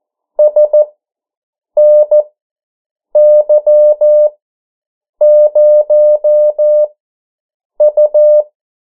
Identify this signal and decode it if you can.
Morse code, heard as a keyed tone, about 14 wpm, spaced as for 7 wpm, 600 Hz. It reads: SNY0U
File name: morse mp3 / 09 urok 8